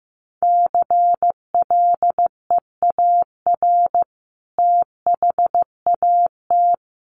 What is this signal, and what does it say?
Morse code, heard as a keyed tone, about 15 wpm, 700 Hz. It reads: CLEAR THAT
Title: Sentences from Top 600 Words 15wpm